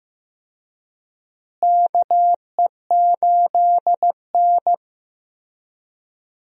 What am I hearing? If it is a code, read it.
Morse code, heard as a keyed tone, about 16 wpm, 700 Hz. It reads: KE8N
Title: USA Call Signs - 4 Characters - Repeated 3x - 15wpm